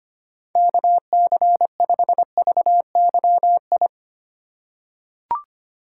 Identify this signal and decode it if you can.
Morse code, heard as a keyed tone, about 25 wpm, 700 Hz. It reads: KC5VYI
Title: USA Call Signs - 6 Characters - 4x Character Spacing - 25wpm